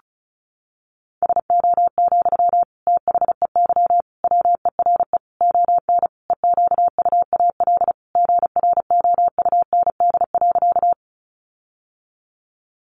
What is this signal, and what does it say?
Morse code, heard as a keyed tone, about 34 wpm, 700 Hz. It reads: SO, THEY WERE ON EQUAL GROUND.
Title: Sentences from Top 1400 Words 35wpm